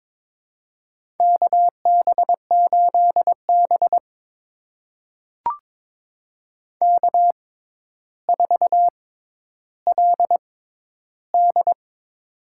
Morse code, heard as a keyed tone, about 21 wpm, spaced as for 11 wpm, 700 Hz. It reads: KB8B K4LD
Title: USA Call Signs - 4 Characters - 6x Character Spacing - 22wpm